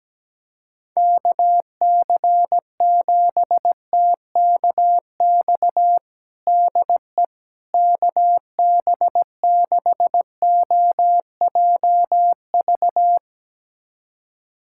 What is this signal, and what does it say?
Morse code, heard as a keyed tone, about 17 wpm, 700 Hz. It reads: KC7TKX DE KB6OJV
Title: USA Callsign DE Callsign - 6 Characters - 17wpm